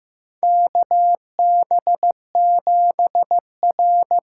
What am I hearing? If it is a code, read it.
Morse code, heard as a keyed tone, about 15 wpm, 700 Hz. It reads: KB7R
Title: USA Call Signs - 6 Characters - 6x Character Spacing - 15wpm